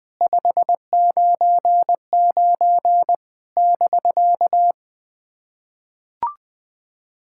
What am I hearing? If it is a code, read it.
Morse code, heard as a keyed tone, about 20 wpm, 700 Hz. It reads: 599 <BK>
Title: SOTA QSO Practice 20wpm